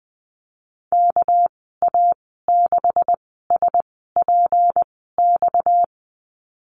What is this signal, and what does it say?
Morse code, heard as a keyed tone, about 20 wpm, spaced as for 14 wpm, 700 Hz. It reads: KA6SPX
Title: 245 Lesson 39g - Letter z - Call Signs 20wpm